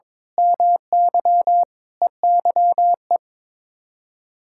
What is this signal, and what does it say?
Morse code, heard as a keyed tone, about 22 wpm, 700 Hz. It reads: MY EYE
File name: practice / 22wpm / Sets of 7 Words - 22wpm